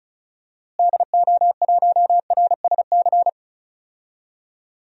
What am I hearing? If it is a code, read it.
Morse code, heard as a keyed tone, about 36 wpm, 700 Hz. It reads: DO1RSC
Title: DXCC-100 Call Signs - 6 Characters - Repeated 2x - 35wpm